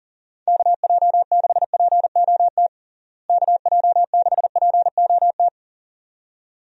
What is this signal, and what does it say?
Morse code, heard as a keyed tone, about 40 wpm, 700 Hz. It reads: KJ6POT KJ6POT
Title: USA Call Signs - 6 Characters - Repeated 2x - 40wpm